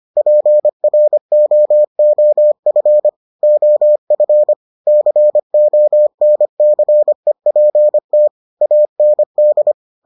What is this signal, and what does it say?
Morse code, heard as a keyed tone, about 25 wpm, 600 Hz. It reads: PROOF OF CONCEPT AND